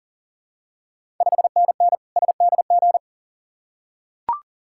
Morse code, heard as a keyed tone, about 39 wpm, 690 Hz. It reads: 5NN SDG
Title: Ontario QSO Party Exchange 40wpm